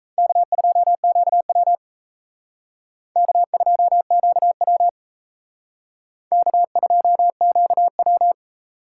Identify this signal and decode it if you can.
Morse code, heard as a keyed tone, about 35 wpm, 700 Hz. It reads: K2QW K2QW K2QW
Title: Speed Racing - USA 1x2 Call Signs 28wpm